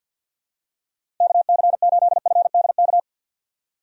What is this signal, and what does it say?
Morse code, heard as a keyed tone, about 50 wpm, 700 Hz. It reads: KC8FDK
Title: USA Call Signs - 6 Characters - Repeated 3x - 50wpm